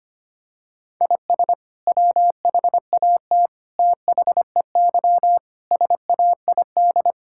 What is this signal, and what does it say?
Morse code, heard as a keyed tone, about 25 wpm, 700 Hz. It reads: IS WHAT THEY SAID
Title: Sentences from Top 1000 Words - Review Part 5 - 25wpm